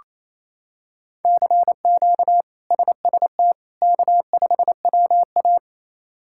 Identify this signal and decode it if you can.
Morse code, heard as a keyed tone, about 28 wpm, 700 Hz. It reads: CQ SST K5WA
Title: K1USN SST QSO Practice - 28wpm